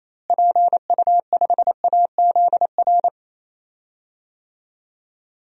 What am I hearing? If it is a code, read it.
Morse code, heard as a keyed tone, about 28 wpm, 700 Hz. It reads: PU5AZR
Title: DXCC-100 Call Signs - 6 Characters - Repeated 3x - 28wpm